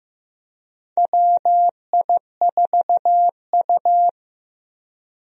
Morse code, heard as a keyed tone, about 15 wpm, 700 Hz. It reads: WI4U
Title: USA Call Signs - 4 Characters - Repeated 3x - 15wpm